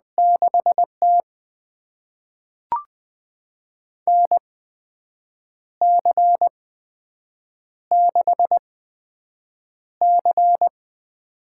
Morse code, heard as a keyed tone, about 20 wpm, spaced as for 7 wpm, 700 Hz. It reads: 6T NC6C
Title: USA Call Signs - 4 Characters - 8x Character Spacing - 20wpm